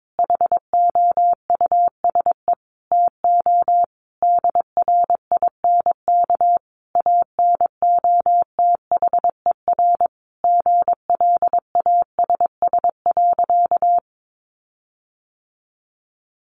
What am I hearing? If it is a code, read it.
Morse code, heard as a keyed tone, about 22 wpm, 700 Hz. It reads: HOUSE TO DRINK ANOTHER GLASS.
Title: Sentences from Top 1500 Words 22wpm